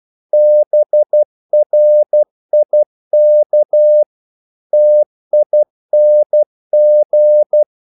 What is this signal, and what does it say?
Morse code, heard as a keyed tone, about 12 wpm, 600 Hz. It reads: BRIK TING
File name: Korte ord DK 12wpm